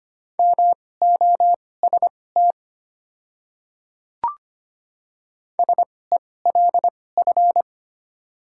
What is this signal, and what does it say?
Morse code, heard as a keyed tone, about 25 wpm, spaced as for 18 wpm, 700 Hz. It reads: MOST SELF
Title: 2x Character Spacing - Top 200 Words - 25wpm